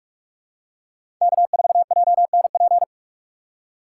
Morse code, heard as a keyed tone, about 41 wpm, 700 Hz. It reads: K4JNP
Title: USA Call Signs - 2x Character Spacing - 45wpm